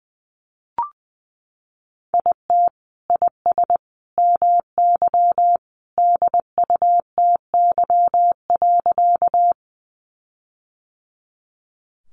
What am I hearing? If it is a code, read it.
Morse code, heard as a keyed tone, about 20 wpm, 700 Hz. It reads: IT IS MY DUTY.
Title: Sentences from Top 1500 Words 20wpm